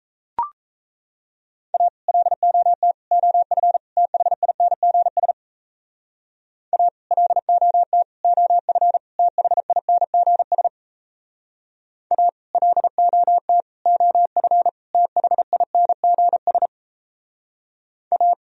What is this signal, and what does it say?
Morse code, heard as a keyed tone, about 34 wpm, 700 Hz. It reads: A LOT OF THINGS A LOT OF THINGS A LOT OF THINGS A
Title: Speed-Racing - Sets of 4 Words 28wpm